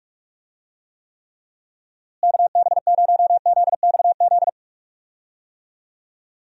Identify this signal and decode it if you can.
Morse code, heard as a keyed tone, about 46 wpm, 700 Hz. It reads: KB0ZXZ
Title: USA Call Signs - 6 Characters - Repeated 3x - 45wpm